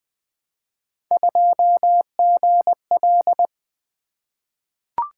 Morse code, heard as a keyed tone, about 20 wpm, 700 Hz. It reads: 2GL
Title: Three Letter-Number Combinations - 20wpm